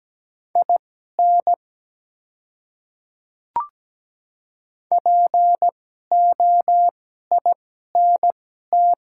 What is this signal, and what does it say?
Morse code, heard as a keyed tone, about 17 wpm, spaced as for 12 wpm, 700 Hz. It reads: IN POINT
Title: 2x Character Spacing - Top 200 Words - 17wpm